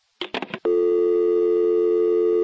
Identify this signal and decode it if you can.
Dial tone (North America), 350 + 440 Hz.
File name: phone numbers / calls / toluca security var2b